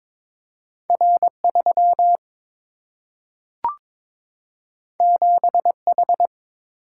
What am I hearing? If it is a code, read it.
Morse code, heard as a keyed tone, about 22 wpm, 700 Hz. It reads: R3 7H
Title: Single-Letter Single-Number 22wpm